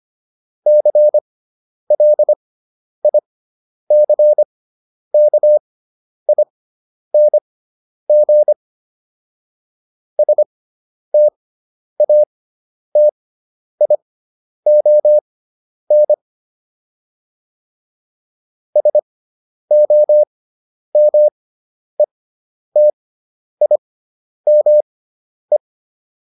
Morse code, heard as a keyed tone, about 25 wpm, spaced as for 10 wpm, 600 Hz. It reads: CLICKING STATION SOMETIME